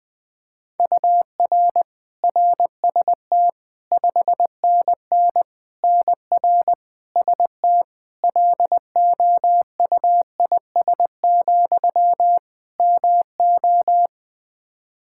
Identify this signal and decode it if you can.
Morse code, heard as a keyed tone, about 20 wpm, 700 Hz. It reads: UR RST 5NN NR ST LOUIS, MO
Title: Practice QSO - Exchange 1 - Beginning - 20wpm